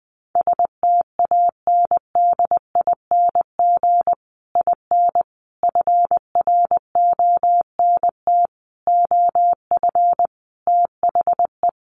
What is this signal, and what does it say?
Morse code, heard as a keyed tone, about 20 wpm, 700 Hz. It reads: STANDING IN FRONT OF THE
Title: Sets of 5 Words Encore2 20wpm